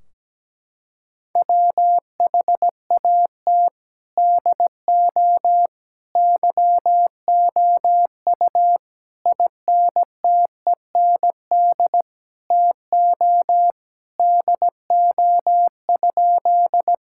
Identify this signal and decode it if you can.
Morse code, heard as a keyed tone, about 17 wpm, 700 Hz. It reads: WHAT DO YOU INTEND TO DO?
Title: Sentences from Top 1600 Words 17wpm